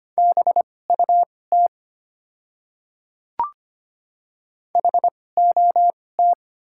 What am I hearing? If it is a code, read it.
Morse code, heard as a keyed tone, about 25 wpm, spaced as for 18 wpm, 700 Hz. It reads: BUT HOT
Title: 2x Character Spacing - Top 100 Words - 25wpm